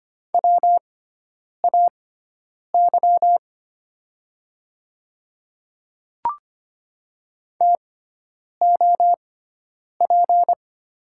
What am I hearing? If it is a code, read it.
Morse code, heard as a keyed tone, about 25 wpm, spaced as for 9 wpm, 700 Hz. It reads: WAY TOP
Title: 6x Character Spacing - Top 500 Words - 25wpm